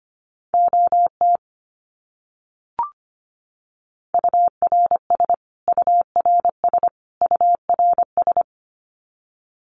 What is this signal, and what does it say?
Morse code, heard as a keyed tone, about 25 wpm, 700 Hz. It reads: OT URS URS URS
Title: Abbreviations Repeated 3x - 25wpm